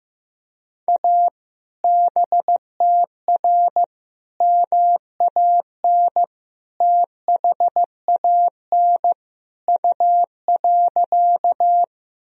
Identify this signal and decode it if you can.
Morse code, heard as a keyed tone, about 16 wpm, 700 Hz. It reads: A BTR MAN THAN U.
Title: Sentences from Top 400 Words and Abbreviations 15wpm